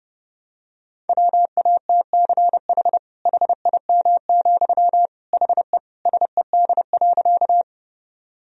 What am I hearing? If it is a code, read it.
Morse code, heard as a keyed tone, about 30 wpm, 700 Hz. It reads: WATCH HIM, HE SED.
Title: Sentences from Top 300 Words and Abbreviations 30wpm